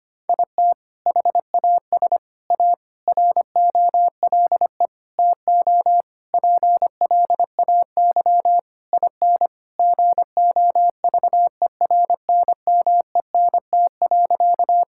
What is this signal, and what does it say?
Morse code, heard as a keyed tone, about 25 wpm, 700 Hz. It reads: IT HAS A ROLE TO PLAY IN GOVERNMENT.
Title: Sentences from Top 1000 Words - Review Part 6 - 25wpm